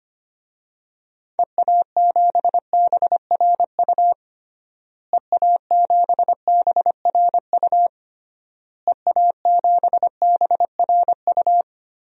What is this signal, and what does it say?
Morse code, heard as a keyed tone, about 25 wpm, 700 Hz. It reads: EA7BRU EA7BRU EA7BRU
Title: DXCC-100 Call Signs - 6 Characters - Repeated 3x - 25wpm